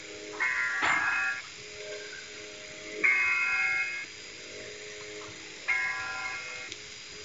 Emergency Alert System (SAME) tones (header bursts), heard as harsh buzzing bursts. It is heard from 0.3 to 6.7 s.